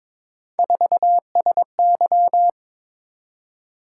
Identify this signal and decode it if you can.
Morse code, heard as a keyed tone, about 22 wpm, 700 Hz. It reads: 4SY
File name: practice / 22wpm / Three Letter-Number Combinations - 22wpm